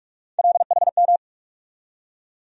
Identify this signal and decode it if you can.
Morse code, heard as a keyed tone, about 42 wpm, 700 Hz. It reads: LHM
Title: USA Call Sign Postfixes - 3 Letters 45wpm